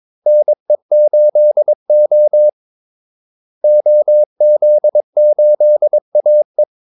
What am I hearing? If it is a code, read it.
Morse code, heard as a keyed tone, about 22 wpm, 600 Hz. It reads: NE8O OZ8AE
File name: Callsigns-22wpm